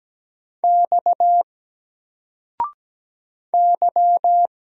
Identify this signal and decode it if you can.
Morse code, heard as a keyed tone, about 17 wpm, 700 Hz. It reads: X Y
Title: Single Letter-Number 17wpm